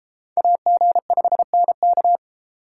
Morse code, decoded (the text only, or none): AG5NK